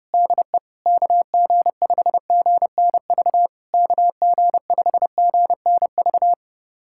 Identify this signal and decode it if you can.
Morse code, heard as a keyed tone, about 30 wpm, 660 Hz. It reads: DE KG5GNV KG5GNV
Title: SOTA QSO Practice 30wpm